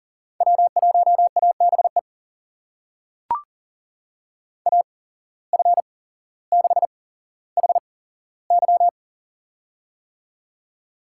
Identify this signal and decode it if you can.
Morse code, heard as a keyed tone, about 40 wpm, spaced as for 17 wpm, 700 Hz. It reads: W1ABE AF6HY
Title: USA Call Signs - 5 Characters - 8x Character Spacing - 40wpm